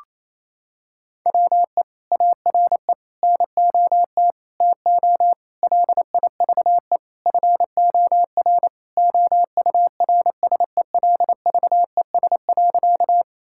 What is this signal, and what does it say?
Morse code, heard as a keyed tone, about 28 wpm, 700 Hz. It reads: WE ARE NOT TO LIVE FOR OURSELVES.
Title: Sentences from Top 1600 Words 28wpm